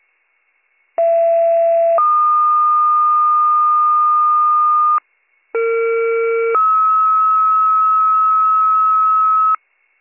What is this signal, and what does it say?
Two-tone page, heard as a one-second tone, then a three-second tone, about 670 then 1220 Hz, about 460 then 1290 Hz. It is heard from 1.0 to 9.6 s.